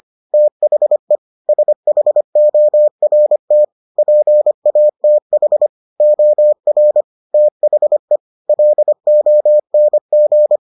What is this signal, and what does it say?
Morse code, heard as a keyed tone, about 25 wpm, 600 Hz. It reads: THE SHORT PATH OR THE LONG